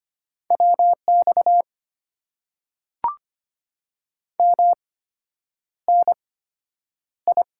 Morse code, heard as a keyed tone, about 25 wpm, spaced as for 9 wpm, 690 Hz. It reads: WX MNI
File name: Top 90 QSO Elements - 8x Character Spacing - 25wpm